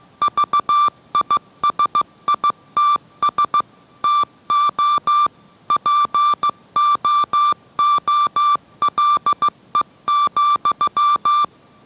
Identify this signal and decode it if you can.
Morse code, heard as a keyed tone, about 15 wpm, 1180 Hz. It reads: VISITS TO POOLE,